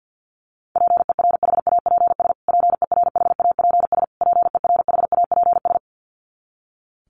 Morse code, heard as a keyed tone, about 49 wpm, 700 Hz. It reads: PERHAPS PERHAPS PERHAPS
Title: Top 500 Words Repeated 50wpm